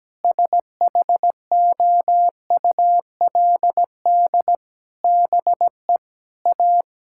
Morse code, heard as a keyed tone, about 17 wpm, 700 Hz. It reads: SHOULD BE A
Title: Sentences from Top 1600 Words 17wpm